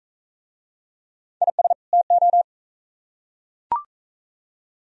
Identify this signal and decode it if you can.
Morse code, heard as a keyed tone, about 40 wpm, 700 Hz. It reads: IS TO